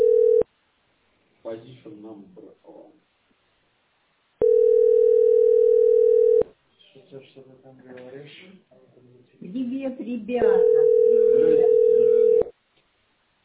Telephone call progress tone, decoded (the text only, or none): ringback tone